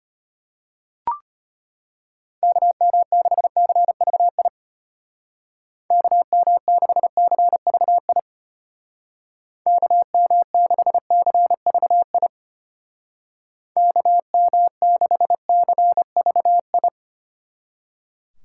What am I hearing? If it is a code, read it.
Morse code, heard as a keyed tone, about 30 wpm, 700 Hz. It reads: KM6CVI KM6CVI KM6CVI KM6CVI
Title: Speed Racing - USA 2x3 Call Signs 25wpm